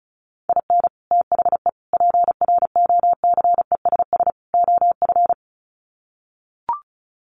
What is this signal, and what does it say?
Morse code, heard as a keyed tone, about 34 wpm, 700 Hz. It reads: IN THE PROCESS OF
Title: Sets of 4 Words 35wpm